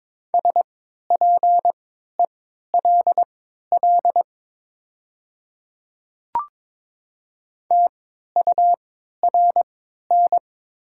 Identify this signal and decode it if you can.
Morse code, heard as a keyed tone, about 22 wpm, spaced as for 12 wpm, 700 Hz. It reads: SPELL TURN
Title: 3x Character Spacing - Top 300 Words - 22wpm